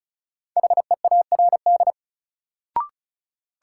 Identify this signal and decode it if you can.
Morse code, heard as a keyed tone, about 36 wpm, 700 Hz. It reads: HEARD
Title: Sentences from Top 400 Words - New Word Drill - 35wpm